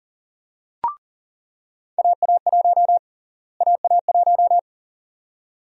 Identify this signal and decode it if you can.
Morse code, heard as a keyed tone, about 40 wpm, 700 Hz. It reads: AA1 AA1
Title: USA Call Sign Prefixes - Repeated 2x - 40wpm